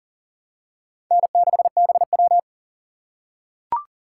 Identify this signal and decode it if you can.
Morse code, heard as a keyed tone, about 40 wpm, 700 Hz. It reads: N6BW
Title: USA Call Signs - 4 Characters - 2x Character Spacing - 40wpm